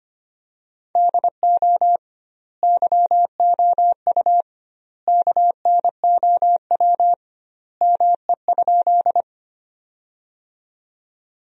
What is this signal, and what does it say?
Morse code, heard as a keyed tone, about 25 wpm, 700 Hz. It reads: DO YOU KNOW ME?
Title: 2x Word-Spacing - Sentences from Top 200 Words - 25wpm